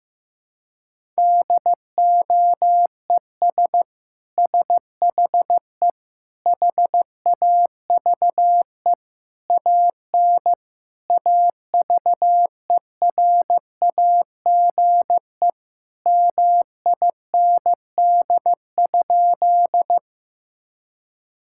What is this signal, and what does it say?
Morse code, heard as a keyed tone, about 15 wpm, 700 Hz. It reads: DOES SHE HAVE AN AVERAGE MIND?
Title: Sentences from Top 1000 Words - Review Part 6 - 15wpm